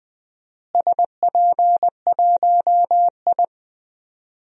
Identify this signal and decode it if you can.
Morse code, heard as a keyed tone, about 20 wpm, 700 Hz. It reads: SP1I
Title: DXCC-100 Call Signs - 4 Characters - Repeated 2x - 20wpm